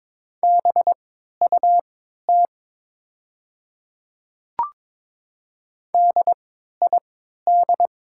Morse code, heard as a keyed tone, about 23 wpm, spaced as for 13 wpm, 700 Hz. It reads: BUT DID
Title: 3x Character Spacing - Top 100 Words - 22wpm